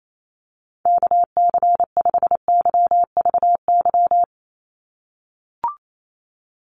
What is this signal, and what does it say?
Morse code, heard as a keyed tone, about 28 wpm, 700 Hz. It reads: KC5YVY
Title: USA Call Signs 28wpm